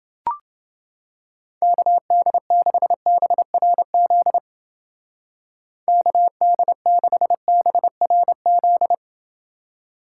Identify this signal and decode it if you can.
Morse code, heard as a keyed tone, about 29 wpm, 700 Hz. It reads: KD6BRZ KD6BRZ